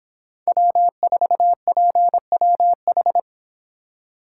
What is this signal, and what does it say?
Morse code, heard as a keyed tone, about 26 wpm, 700 Hz. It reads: W4PWH